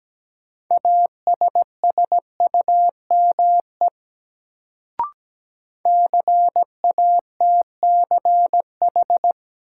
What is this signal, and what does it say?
Morse code, heard as a keyed tone, about 17 wpm, 700 Hz. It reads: ASSUME CATCH
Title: Sentences from Top 1500 Words - New Word Drill Alt - 17wpm